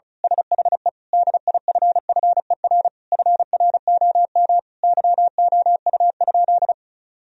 Morse code, heard as a keyed tone, about 36 wpm, 700 Hz. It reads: SHE DIFFER FROM YOU?
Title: Sentences from Top 1000 Words - Review Part 1 - 35wpm